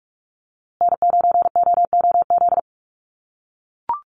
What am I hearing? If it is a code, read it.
Morse code, heard as a keyed tone, about 44 wpm, 700 Hz. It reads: N9OOZ
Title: USA Call Signs Repeated 3x 45wpm